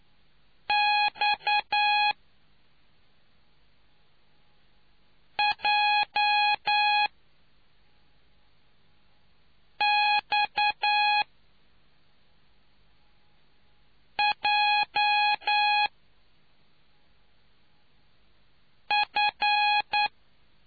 Morse code, heard as a keyed tone, about 9 wpm, spaced as for 3 wpm, 830 Hz. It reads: XJXJF